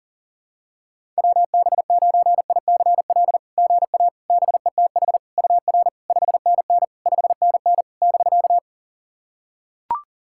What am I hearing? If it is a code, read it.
Morse code, heard as a keyed tone, about 40 wpm, 690 Hz. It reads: WB9ICL GA BETH UR 5NN 5NN <BK>
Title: SOTA QSO Practice 40wpm